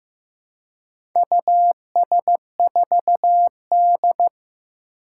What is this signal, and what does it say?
Morse code, heard as a keyed tone, about 15 wpm, 700 Hz. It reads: US4D
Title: Four Challenging Letter-Number Combinations 15wpm